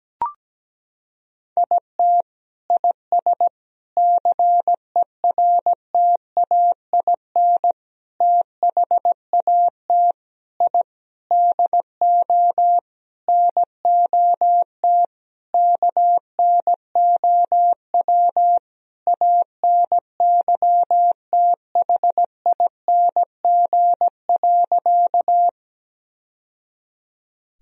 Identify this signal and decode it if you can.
Morse code, heard as a keyed tone, about 17 wpm, 700 Hz. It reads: IT IS CERTAIN THAT I DO NOT KNOW ANYTHING.
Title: Sentences from Top 1000 Words - Review Part 2 - 17wpm